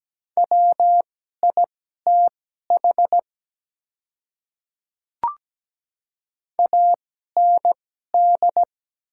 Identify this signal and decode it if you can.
Morse code, heard as a keyed tone, about 17 wpm, spaced as for 12 wpm, 700 Hz. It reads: WITH AND
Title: 2x Character Spacing - Top 100 Words - 17wpm